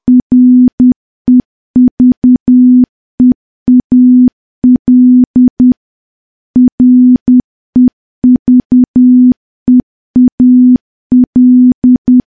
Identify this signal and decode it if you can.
Morse code, heard as a keyed tone, about 10 wpm, 250 Hz. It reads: REVEAL REVEAL
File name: morse / morse-training-10WPM